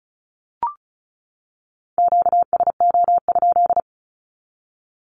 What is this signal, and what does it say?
Morse code, heard as a keyed tone, about 34 wpm, 700 Hz. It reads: QSO?
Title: Q-Codes 35wpm